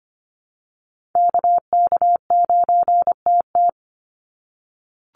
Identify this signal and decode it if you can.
Morse code, heard as a keyed tone, about 25 wpm, 700 Hz. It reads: KK9TT
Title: USA Call Signs 25wpm